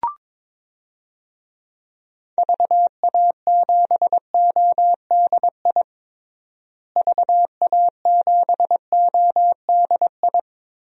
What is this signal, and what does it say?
Morse code, heard as a keyed tone, about 22 wpm, 700 Hz. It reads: VA7ODI VA7ODI
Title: DXCC-100 Call Signs - 6 Characters - Repeated 2x - 22wpm